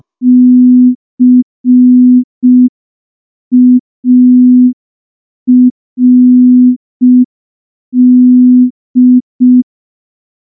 Morse code, heard as a keyed tone, about 5 wpm, 250 Hz. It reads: CARD
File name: morse / morse-training-05WPM